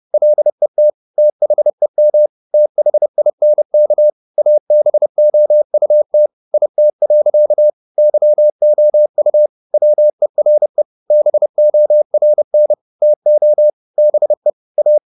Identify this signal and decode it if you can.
Morse code, heard as a keyed tone, about 30 wpm, 600 Hz. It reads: LET THEM THINK ABOUT IT. YOU WERE BORN TO BE A